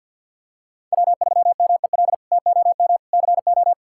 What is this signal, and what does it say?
Morse code, heard as a keyed tone, about 50 wpm, 690 Hz. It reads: W3MEL TOM CO
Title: K1USN SST QSO Practice - 50wpm